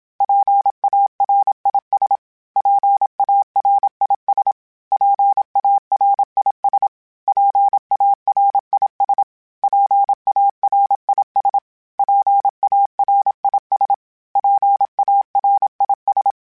Morse code, heard as a keyed tone, about 27 wpm, 800 Hz. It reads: PARIS PARIS PARIS PARIS PARIS PARIS PARIS
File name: PARIS Calibration / PARIS mp3s / PARIS@30